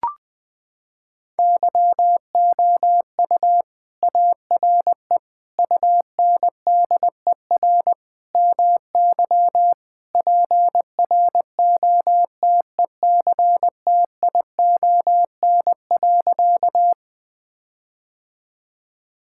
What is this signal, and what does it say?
Morse code, heard as a keyed tone, about 20 wpm, 700 Hz. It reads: YOU ARE UNDER MY PROTECTION.
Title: Sentences from Top 1600 Words 20wpm